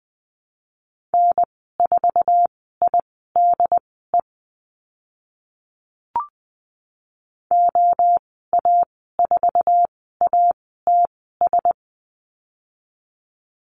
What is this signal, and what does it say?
Morse code, heard as a keyed tone, about 20 wpm, spaced as for 15 wpm, 700 Hz. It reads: N4IDE OA4ATS
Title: 082 Lesson 14g - Number 2 - Call Signs 20wpm